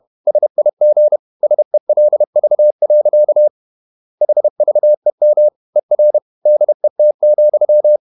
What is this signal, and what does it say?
Morse code, heard as a keyed tone, about 31 wpm, 600 Hz. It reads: SIG SELV. HVEM ER DET,
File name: Ferie QRQ 31wpm